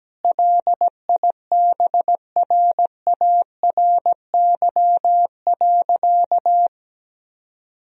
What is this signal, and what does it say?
Morse code, heard as a keyed tone, about 17 wpm, 700 Hz. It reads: LIBRARY.
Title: Sentences from Top 1000 Words - Review Part 4 - 17wpm